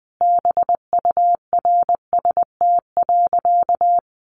Morse code, heard as a keyed tone, about 20 wpm, 700 Hz. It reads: BURST.
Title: A Sound Of Thunder 20wpm